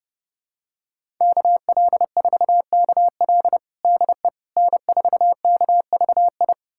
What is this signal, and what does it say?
Morse code, heard as a keyed tone, about 30 wpm, 700 Hz. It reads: KL4KL DE N4KVI
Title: USA Callsign DE Callsign - 5 Characters - 30wpm